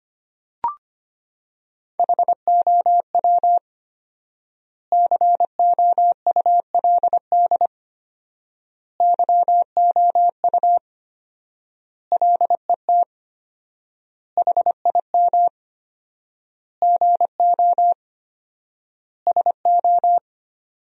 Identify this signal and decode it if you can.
Morse code, heard as a keyed tone, about 25 wpm, 700 Hz. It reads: HOW COULD YOU LET HIM GO SO
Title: 4x Word-Spacing - Sentences from Top 300 Words - 25wpm